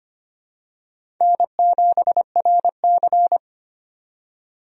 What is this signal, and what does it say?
Morse code, heard as a keyed tone, about 25 wpm, 700 Hz. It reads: N7RC
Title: USA Call Signs - 4 Characters - 25wpm